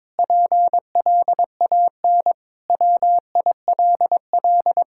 Morse code, heard as a keyed tone, about 22 wpm, 700 Hz. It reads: PLAN WILL
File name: practice / 22wpm / Sentences with 8-Word Phrases - 22wpm